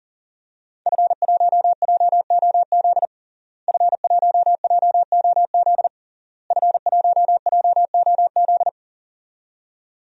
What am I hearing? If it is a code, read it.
Morse code, heard as a keyed tone, about 37 wpm, 700 Hz. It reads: F1JOZ F1JOZ F1JOZ
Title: DXCC-100 Call Signs - 5 Characters - Repeated 3x - 40wpm